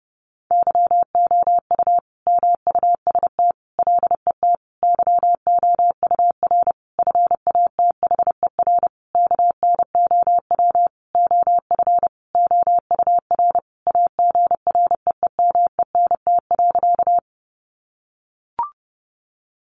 Morse code, heard as a keyed tone, about 30 wpm, 700 Hz. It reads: YOU MUST LET YOUR FATHER KNOW OF OUR AGREEMENT.
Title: Sentences from Top 1500 Words 30wpm